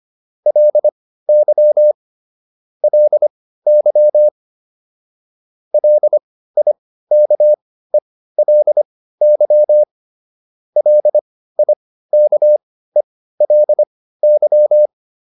Morse code, heard as a keyed tone, about 25 wpm, spaced as for 15 wpm, 600 Hz. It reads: LY LY LIKELY LIKELY